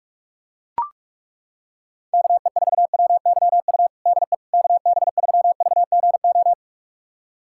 Morse code, heard as a keyed tone, about 41 wpm, 700 Hz. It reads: KE4WYU DE KB3VGO
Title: USA Callsign DE Callsign - 6 Characters - 45wpm